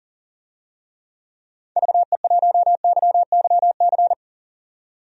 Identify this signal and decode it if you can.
Morse code, heard as a keyed tone, about 39 wpm, 700 Hz. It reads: VE1YYC
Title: DXCC-100 Call Signs - 6 Characters - 40wpm